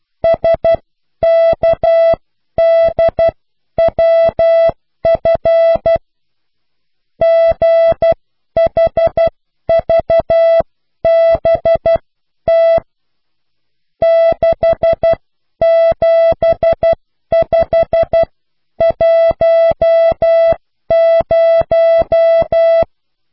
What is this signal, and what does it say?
Morse code, heard as a keyed tone, about 12 wpm, spaced as for 10 wpm, 660 Hz. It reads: SKDWF GHVBT 67510